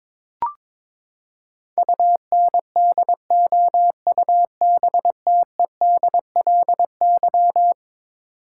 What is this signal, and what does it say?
Morse code, heard as a keyed tone, about 22 wpm, 700 Hz. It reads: UNDOUBTEDLY
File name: practice / 22wpm / Eleven Letter Words - 22wpm